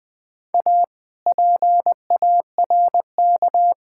Morse code, heard as a keyed tone, about 20 wpm, 700 Hz. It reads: A PARK
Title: Sentences with 5-Word Phrases - 20wpm